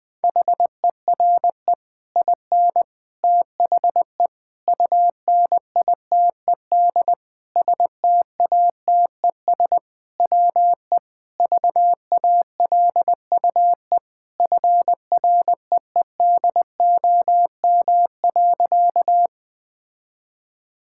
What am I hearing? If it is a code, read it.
Morse code, heard as a keyed tone, about 20 wpm, 700 Hz. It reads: HERE IN THE UNITED STATES WE VALUE FREEDOM.
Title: Sentences with 5-Word Phrases - 20wpm